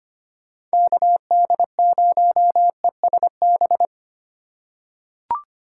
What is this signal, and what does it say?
Morse code, heard as a keyed tone, about 25 wpm, 700 Hz. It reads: KD0ESB
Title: USA Call Signs - 6 Characters - 3x Character Spacing - 25wpm